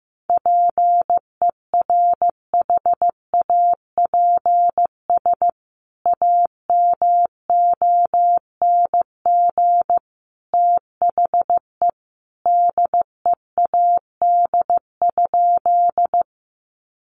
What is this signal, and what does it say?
Morse code, heard as a keyed tone, about 14 wpm, 700 Hz. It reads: PERHAPS AMONG THE DEAD?
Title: Sentences from Top 1500 Words 15wpm